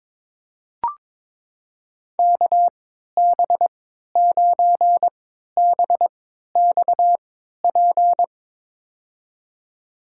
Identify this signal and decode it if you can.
Morse code, heard as a keyed tone, about 22 wpm, spaced as for 12 wpm, 700 Hz. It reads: KB9BXP
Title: USA Call Signs - 6 Characters - 3x Character Spacing - 22wpm